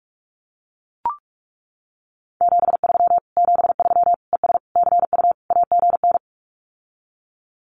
Morse code, heard as a keyed tone, about 44 wpm, 700 Hz. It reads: 73 73 ES CU AGN
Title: Common QSO Phrases Encore 45wpm